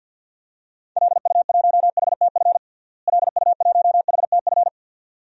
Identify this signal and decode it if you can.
Morse code, heard as a keyed tone, about 47 wpm, 700 Hz. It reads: LU1HTF LU1HTF
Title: DXCC-100 Call Signs - 6 Characters - Repeated 2x - 50wpm